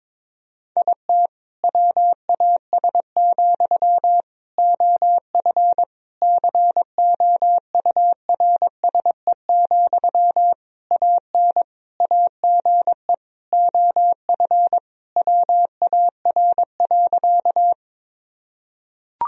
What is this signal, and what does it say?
Morse code, heard as a keyed tone, about 22 wpm, 700 Hz. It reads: IT WAS, OF COURSE, AN AGE OF WAR.
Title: Sentences from Top 1000 Words - Review Part 3 - 22wpm